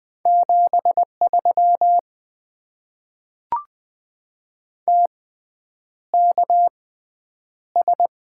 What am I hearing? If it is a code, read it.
Morse code, heard as a keyed tone, about 20 wpm, spaced as for 9 wpm, 700 Hz. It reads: 73 TKS
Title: Top 90 QSO Elements - 6x Character Spacing - 20wpm